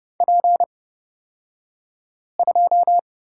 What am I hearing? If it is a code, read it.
Morse code, heard as a keyed tone, about 30 wpm, spaced as for 5 wpm, 700 Hz. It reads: P2
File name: Lesson 38d - Letter q - Two Characters 30wpm